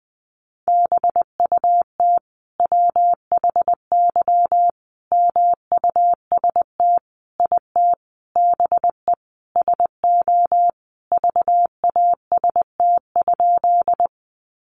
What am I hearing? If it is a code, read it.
Morse code, heard as a keyed tone, about 19 wpm, 700 Hz. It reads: BUT WHY MUST IT BE SO VAST?
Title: Sentences from Top 1300 Words 20wpm